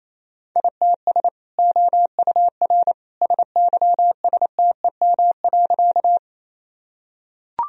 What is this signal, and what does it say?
Morse code, heard as a keyed tone, about 28 wpm, 700 Hz. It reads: ITS OUR SYSTEM.
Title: Sentences from Top 1000 Words - Review Part 3 - 28wpm